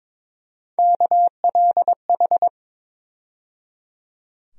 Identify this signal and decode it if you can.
Morse code, heard as a keyed tone, about 22 wpm, 700 Hz. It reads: KLH
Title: USA Call Sign Postfixes - 3 Letters 22wpm